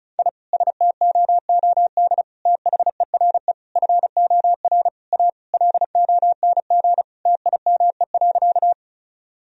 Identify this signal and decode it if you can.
Morse code, heard as a keyed tone, about 36 wpm, 700 Hz. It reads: I STOOD THERE FOR A LONG TIME.
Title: Sentences from Top 1000 Words - Review Part 2 - 35wpm